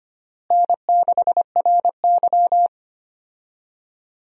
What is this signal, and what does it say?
Morse code, heard as a keyed tone, about 25 wpm, 700 Hz. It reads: N6RY
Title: USA Call Signs - 4 Characters - Repeated 3x - 25wpm